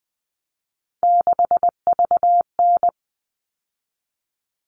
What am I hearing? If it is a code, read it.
Morse code, heard as a keyed tone, about 20 wpm, 700 Hz. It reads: 6VN
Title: All Cut Numbers - 3-Digit 20wpm